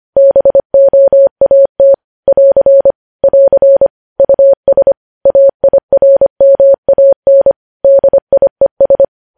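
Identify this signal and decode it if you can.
Morse code, heard as a keyed tone, about 25 wpm, 550 Hz. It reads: BOAT <AR> <AR> US AIRMAN DIES
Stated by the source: Morse code news headlines at 25 wpm.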